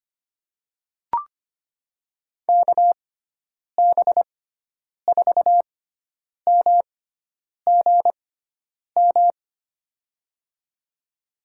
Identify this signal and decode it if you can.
Morse code, heard as a keyed tone, about 25 wpm, spaced as for 9 wpm, 700 Hz. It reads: KB4MGM
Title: USA Call Signs - 6 Characters - 6x Character Spacing - 25wpm